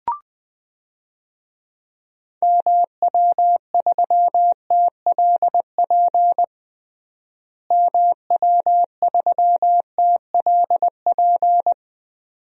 Morse code, heard as a keyed tone, about 20 wpm, 700 Hz. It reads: MW3TLP MW3TLP
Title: DXCC-100 Call Signs - 6 Characters - Repeated 2x - 20wpm